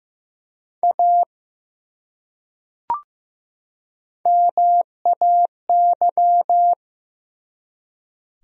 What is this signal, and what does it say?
Morse code, heard as a keyed tone, about 15 wpm, 700 Hz. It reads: A MAY
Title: Top 100 Words 15wpm